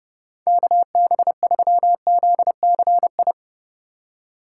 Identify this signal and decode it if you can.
Morse code, heard as a keyed tone, about 38 wpm, 700 Hz. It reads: KB3ZCI
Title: Speed Racing - USA 2x3 Call Signs 30wpm